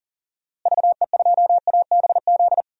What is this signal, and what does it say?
Morse code, heard as a keyed tone, about 40 wpm, 700 Hz. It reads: VE2ABZ
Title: DXCC-100 Call Signs - 6 Characters - Repeated 2x - 40wpm